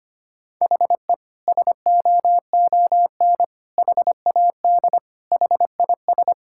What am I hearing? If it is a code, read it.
Morse code, heard as a keyed tone, about 25 wpm, 700 Hz. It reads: HE SOON HAD HIS
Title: Sentences from Top 1600 Words 25wpm